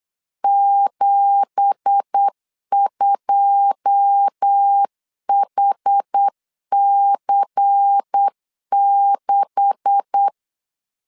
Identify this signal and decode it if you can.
Morse code, heard as a keyed tone, about 8 wpm, 800 Hz. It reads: 72HC6
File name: LEZIONI / 10 SIGNS / 10D COMA 8wpm 3749